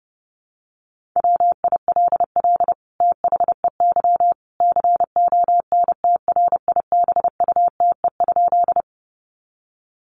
Morse code, heard as a keyed tone, about 30 wpm, 700 Hz. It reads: WILL THEY CONTRIBUTE?
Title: Sentences from Top 1400 Words 30wpm